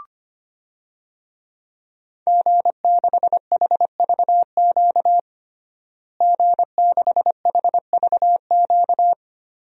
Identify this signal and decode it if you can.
Morse code, heard as a keyed tone, about 25 wpm, 700 Hz. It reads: G6HVQ G6HVQ
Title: DXCC-100 Call Signs - 5 Characters - Repeated 3x - 25wpm